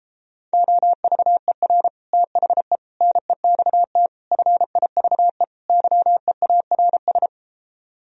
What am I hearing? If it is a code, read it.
Morse code, heard as a keyed tone, about 33 wpm, 700 Hz. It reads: OVER THE NEXT FIVE YEARS